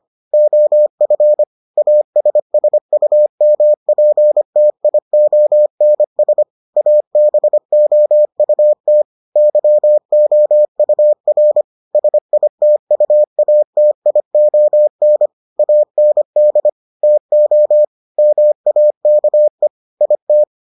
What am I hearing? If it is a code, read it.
Morse code, heard as a keyed tone, about 25 wpm, 600 Hz. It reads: OF ASSUMPTIONS ABOUT YOUR SITUATION AND TO MAKE IT